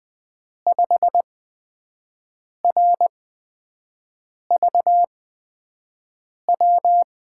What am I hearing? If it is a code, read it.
Morse code, heard as a keyed tone, about 20 wpm, spaced as for 5 wpm, 700 Hz. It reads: 5RVW
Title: USA Call Signs - 8x Character Spacing - 20wpm